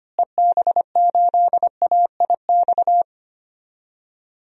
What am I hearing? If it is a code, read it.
Morse code, heard as a keyed tone, about 25 wpm, 700 Hz. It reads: EB8AIX
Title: DXCC-100 Call Signs - 6 Characters - Repeated 3x - 25wpm